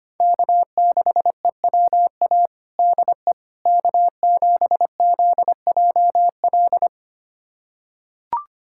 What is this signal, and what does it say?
Morse code, heard as a keyed tone, about 25 wpm, 700 Hz. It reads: K6EWA DE K7ZJL
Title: USA Callsign DE Callsign - 5 Characters - 25wpm